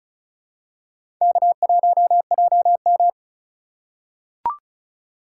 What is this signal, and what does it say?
Morse code, heard as a keyed tone, about 36 wpm, 700 Hz. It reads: K1JM
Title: USA Call Signs - 4 Characters - 4x Character Spacing - 35wpm